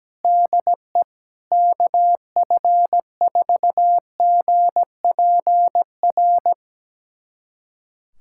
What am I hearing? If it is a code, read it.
Morse code, heard as a keyed tone, about 17 wpm, 700 Hz. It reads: DE KF4GPR
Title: USA Callsign DE Callsign - 6 Characters - 17wpm